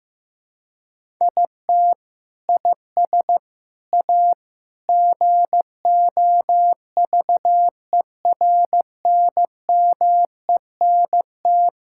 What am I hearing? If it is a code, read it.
Morse code, heard as a keyed tone, about 15 wpm, 700 Hz. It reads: IT IS A GOVERNMENT
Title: Sentences from Top 1000 Words - Review Part 5 - 15wpm